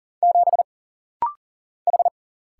Morse code, heard as a keyed tone, about 40 wpm, 700 Hz. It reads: 7 H
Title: Single Letter-Number - Rapid-Fire 40wpm